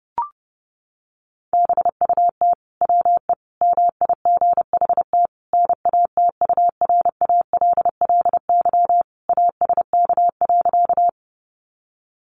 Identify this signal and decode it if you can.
Morse code, heard as a keyed tone, about 30 wpm, 700 Hz. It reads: BUT WE MIGHT NATURALLY ASK.
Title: Sentences from Top 1300 Words 30wpm